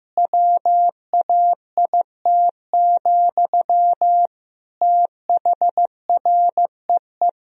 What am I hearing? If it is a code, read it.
Morse code, heard as a keyed tone, about 15 wpm, 700 Hz. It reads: WAIT, THREE
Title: Sentences from Top 1000 Words - Review Part 2 - 15wpm